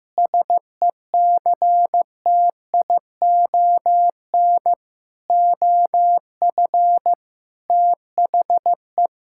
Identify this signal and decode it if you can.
Morse code, heard as a keyed tone, about 15 wpm, 700 Hz. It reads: SECTION OF THE
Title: Sentences from Top 1000 Words - Review Part 5 - 15wpm